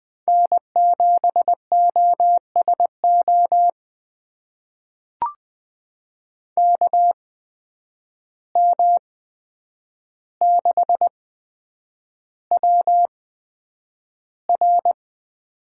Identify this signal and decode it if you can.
Morse code, heard as a keyed tone, about 20 wpm, spaced as for 9 wpm, 700 Hz. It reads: N7OSO KM6WR
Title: USA Call Signs - 5 Characters - 8x Character Spacing - 20wpm